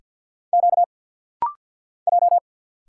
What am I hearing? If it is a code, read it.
Morse code, heard as a keyed tone, about 50 wpm, 700 Hz. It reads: Q J
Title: Single Letters - Rapid-Fire 50wpm